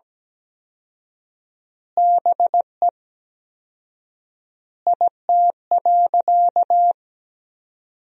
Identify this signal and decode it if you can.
Morse code, heard as a keyed tone, about 17 wpm, 700 Hz. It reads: BE IT.
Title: 4x Word-Spacing - Sentences from Top 100 Words - 17wpm